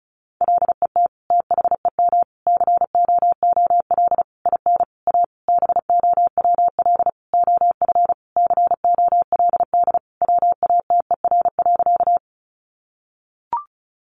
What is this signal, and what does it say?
Morse code, heard as a keyed tone, about 34 wpm, 700 Hz. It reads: LET THEM COOL IN A BOWL OF COLD WATER.
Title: Sentences from Top 1200 Words 35wpm